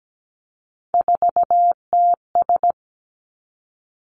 Morse code, heard as a keyed tone, about 17 wpm, 700 Hz. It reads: 4TS
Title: Three Challenging Letter-Number Combinations 17wpm